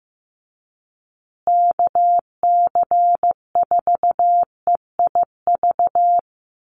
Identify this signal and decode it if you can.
Morse code, heard as a keyed tone, about 15 wpm, 700 Hz. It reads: KC4EIV
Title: USA Call Signs Repeated 3x 15wpm